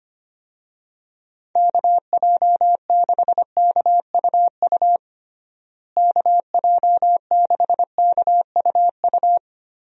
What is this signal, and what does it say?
Morse code, heard as a keyed tone, about 25 wpm, 700 Hz. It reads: KJ6KUU KJ6KUU
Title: USA Call Signs - 6 Characters - Repeated 3x - 25wpm